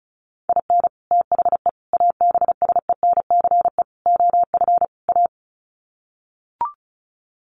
Morse code, heard as a keyed tone, about 34 wpm, 700 Hz. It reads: IN THE ABSENCE OF A
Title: Sets of 5 Words Encore 35wpm